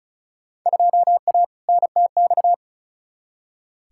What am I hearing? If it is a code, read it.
Morse code, heard as a keyed tone, about 36 wpm, 700 Hz. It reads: 2A NTX
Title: Field Day Exchanges - 35wpm